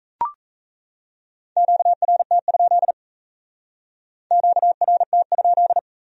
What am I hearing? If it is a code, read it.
Morse code, heard as a keyed tone, about 35 wpm, 700 Hz. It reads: QRT? QRT?
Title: Speed-Racing Q-Codes - 28wpm